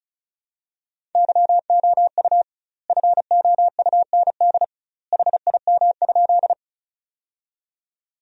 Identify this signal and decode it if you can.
Morse code, heard as a keyed tone, about 34 wpm, 700 Hz. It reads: YOU FOUND HIM?
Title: 2x Word-Spacing - Sentences from Top 300 Words - 35wpm